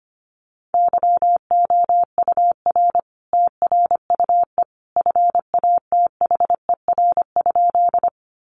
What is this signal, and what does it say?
Morse code, heard as a keyed tone, about 25 wpm, 700 Hz. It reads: YOUR TRUE FATHER?
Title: Sentences from Top 400 Words 25wpm